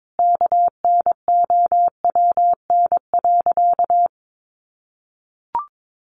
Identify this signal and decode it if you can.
Morse code, heard as a keyed tone, about 22 wpm, 700 Hz. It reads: KNOWN.
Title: Sentences from Top 1100 Words 22wpm